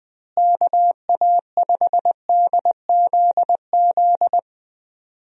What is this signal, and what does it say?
Morse code, heard as a keyed tone, about 21 wpm, 700 Hz. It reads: KA5DZZ
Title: USA Call Signs - 6 Characters - Repeated 2x - 20wpm